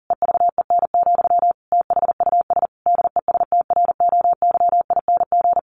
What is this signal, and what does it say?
Morse code, heard as a keyed tone, about 40 wpm, 700 Hz. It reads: EVEN, THUS DESTROYING
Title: A Sound Of Thunder 40wpm